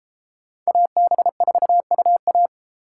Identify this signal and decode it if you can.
Morse code, heard as a keyed tone, about 32 wpm, 700 Hz. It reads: AB4UA